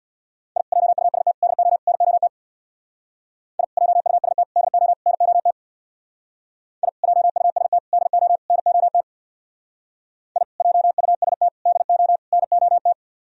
Morse code, heard as a keyed tone, about 63 wpm, 690 Hz. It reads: I JUST DO NOT I JUST DO NOT I JUST DO NOT I JUST DO NOT